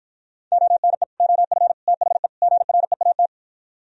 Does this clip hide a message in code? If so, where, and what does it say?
Yes: Morse code from 0.5 to 3.3 s, heard as a keyed tone, about 52 wpm, 700 Hz. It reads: ONE OF THE GREAT